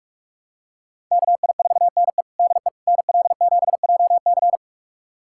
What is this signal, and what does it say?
Morse code, heard as a keyed tone, about 46 wpm, 700 Hz. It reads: KI4NE DE NL7JC
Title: USA Callsign DE Callsign 45wpm